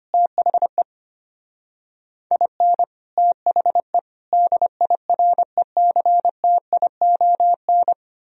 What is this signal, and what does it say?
Morse code, heard as a keyed tone, about 32 wpm, 700 Hz. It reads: THE IN THE DIRECTION